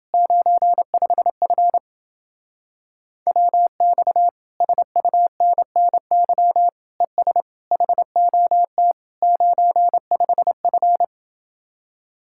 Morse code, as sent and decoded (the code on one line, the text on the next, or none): ----. ..... ..-. / .-- -..- / ... ..- -. -. -.-- / . ... / .... --- - / ----. ..... ..-.
95F WX SUNNY ES HOT 95F